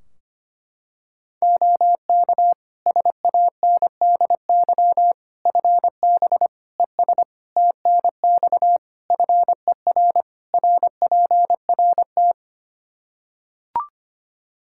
Morse code, heard as a keyed tone, about 25 wpm, 700 Hz. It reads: OK SANDY FB ES TNX FER RPRT
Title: K1USN SST QSO Practice 25wpm